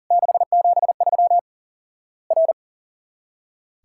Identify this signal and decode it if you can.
Morse code, heard as a keyed tone, about 40 wpm, 700 Hz. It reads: 673 R
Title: Worked All Europe DX Contest - QTC Exchange - 40wpm